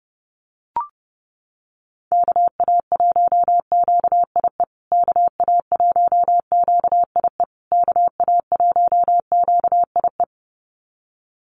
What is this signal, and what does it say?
Morse code, heard as a keyed tone, about 30 wpm, 700 Hz. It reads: KA1QIE KA1QIE KA1QIE
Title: USA Call Signs Repeated 3x 30wpm